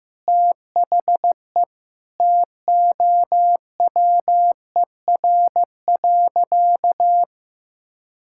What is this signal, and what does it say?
Morse code, heard as a keyed tone, about 15 wpm, 700 Hz. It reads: THE TOWER.
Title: Pangram Challenge v2 - 15wpm